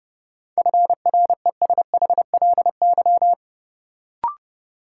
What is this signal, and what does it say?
Morse code, heard as a keyed tone, about 30 wpm, 700 Hz. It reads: FRESHLY
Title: Sets of 5 Words Encore - New Word Drill - 30wpm